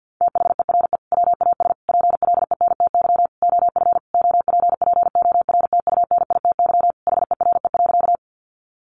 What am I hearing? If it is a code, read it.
Morse code, heard as a keyed tone, about 50 wpm, 700 Hz. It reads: THERE WAS PLENTY OF OPPORTUNITY HERE.
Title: Sentences from Top 1300 Words 50wpm